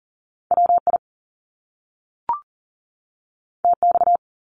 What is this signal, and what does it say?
Morse code, heard as a keyed tone, about 41 wpm, 700 Hz. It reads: WI TX
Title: US States 40wpm